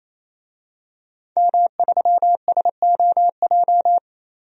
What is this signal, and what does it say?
Morse code, heard as a keyed tone, about 28 wpm, 700 Hz. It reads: M3SOJ
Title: DXCC-100 Call Signs - 5 Characters - 28wpm